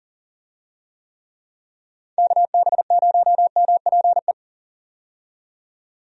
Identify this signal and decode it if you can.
Morse code, heard as a keyed tone, about 40 wpm, 700 Hz. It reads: KB0MPE
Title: USA Call Signs - 6 Characters - 40wpm